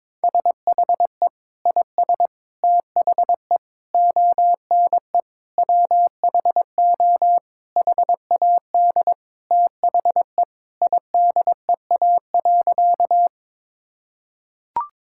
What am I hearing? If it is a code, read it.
Morse code, heard as a keyed tone, about 22 wpm, 700 Hz. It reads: SHE IS THE ONE WHO HAD THE IDEA.
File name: practice / 22wpm / Sentences with 2-Word Phrases - 22wpm